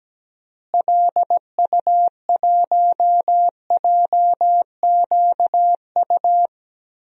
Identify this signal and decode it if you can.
Morse code, heard as a keyed tone, about 17 wpm, 700 Hz. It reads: LU1JQU
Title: DXCC-100 Call Signs - 6 Characters - Repeated 3x - 17wpm